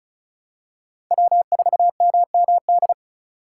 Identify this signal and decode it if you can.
Morse code, heard as a keyed tone, about 36 wpm, 700 Hz. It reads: W4MMD
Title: K1USN SST QSO Practice - 35wpm